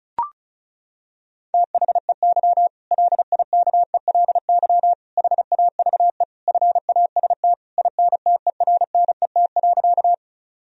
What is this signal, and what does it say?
Morse code, heard as a keyed tone, about 36 wpm, 700 Hz. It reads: THEY LIKELY HAVE FAST INTERNET.
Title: Sentences from Top 1000 Words - Review Part 4 - 35wpm